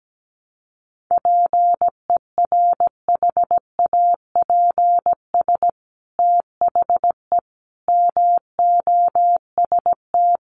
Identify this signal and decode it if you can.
Morse code, heard as a keyed tone, about 17 wpm, 700 Hz. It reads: PERHAPS THE MOST
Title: Sentences from Top 1300 Words 17wpm